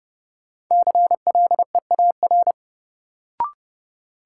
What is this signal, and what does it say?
Morse code, heard as a keyed tone, about 32 wpm, 700 Hz. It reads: CLEAR
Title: Sets of 5 Words Encore - New Word Drill - 30wpm